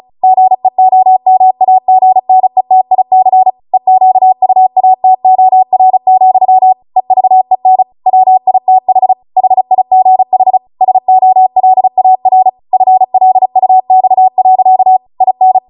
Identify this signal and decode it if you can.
Morse code, heard as a keyed tone, about 36 wpm, 750 Hz. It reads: GEOMAGNETIC EQUATOR, EVEN WITH HIGH SOLAR FLUX. IN